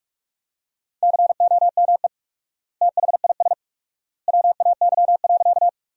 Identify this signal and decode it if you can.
Morse code, heard as a keyed tone, about 47 wpm, 700 Hz. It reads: COME THIS WAY.
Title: 4x Word-Spacing - Sentences from Top 100 Words - 45wpm